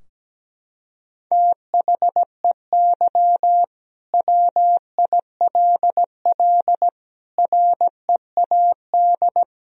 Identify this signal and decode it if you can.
Morse code, heard as a keyed tone, about 17 wpm, 700 Hz. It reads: THEY WILL READ
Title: Sentences from Top 1000 Words - Review Part 4 - 17wpm